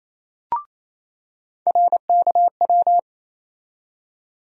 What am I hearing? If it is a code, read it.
Morse code, heard as a keyed tone, about 28 wpm, 700 Hz. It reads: RKW
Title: USA Call Sign Postfixes - 3 Letters 28wpm